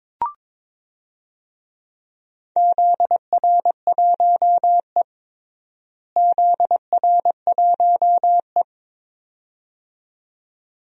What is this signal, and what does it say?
Morse code, heard as a keyed tone, about 22 wpm, 700 Hz. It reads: ZR1E ZR1E
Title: DXCC-100 Call Signs - 4 Characters - Repeated 2x - 22wpm